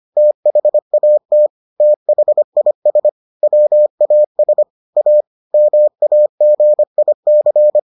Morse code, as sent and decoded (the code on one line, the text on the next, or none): - .... .- - / - .... .. ... / .-- .- ... / .- / -- .- --. .. -.-.
THAT THIS WAS A MAGIC